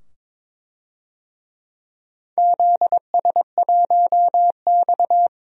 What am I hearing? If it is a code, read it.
Morse code, heard as a keyed tone, about 22 wpm, 700 Hz. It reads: ZS1X
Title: DXCC-100 Call Signs - 4 Characters - 22wpm